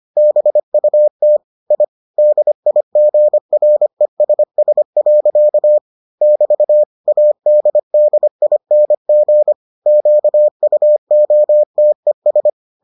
Morse code, heard as a keyed tone, about 25 wpm, 600 Hz. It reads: BUT I DIGRESS. <BT> ADDING QUOTES